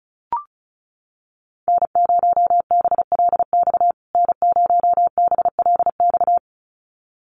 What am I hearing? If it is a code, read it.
Morse code, heard as a keyed tone, about 34 wpm, 700 Hz. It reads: N0BLX N0BLX
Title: USA Call Signs Repeated 2x 35wpm